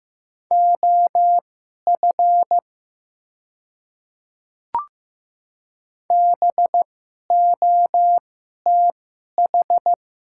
Morse code, heard as a keyed tone, about 15 wpm, spaced as for 11 wpm, 700 Hz. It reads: OF BOTH
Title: 2x Character Spacing - Top 500 Words - 15wpm